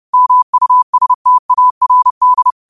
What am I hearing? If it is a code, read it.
Morse code, heard as a keyed tone, about 30 wpm, 1000 Hz. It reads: MUSTARD
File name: qotdfiles / 30wpm